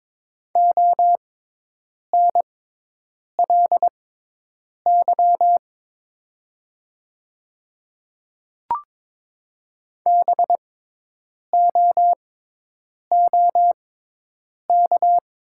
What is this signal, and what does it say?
Morse code, heard as a keyed tone, about 22 wpm, spaced as for 8 wpm, 700 Hz. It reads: ONLY BOOK
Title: 6x Character Spacing - Top 300 Words - 22wpm